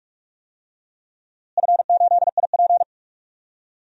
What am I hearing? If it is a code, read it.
Morse code, heard as a keyed tone, about 44 wpm, 700 Hz. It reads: F8IP
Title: DXCC-100 Call Signs - 4 Characters - Repeated 2x - 45wpm